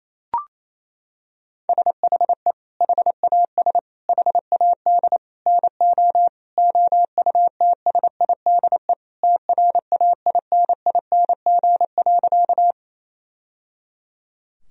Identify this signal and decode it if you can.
Morse code, heard as a keyed tone, about 28 wpm, 700 Hz. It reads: SHE HAS HAD NO OUTSIDE TRAINING.
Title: Sentences from Top 1000 Words - Review Part 4 - 28wpm